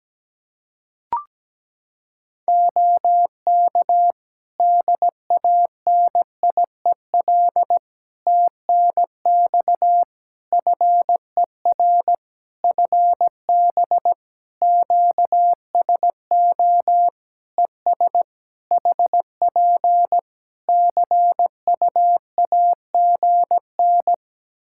Morse code, heard as a keyed tone, about 17 wpm, 700 Hz. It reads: OK DANIEL TNX FER FB QSO ES HP CUAGN
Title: K1USN SST QSO Practice 17wpm